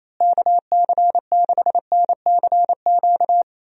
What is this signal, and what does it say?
Morse code, heard as a keyed tone, about 28 wpm, 700 Hz. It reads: KC6NCQ
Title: USA Call Signs - 6 Characters - Repeated 2x - 28wpm